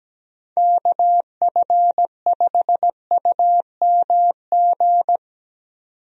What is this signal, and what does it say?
Morse code, heard as a keyed tone, about 17 wpm, 700 Hz. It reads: KF5UMG
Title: USA Call Signs - 6 Characters - 17wpm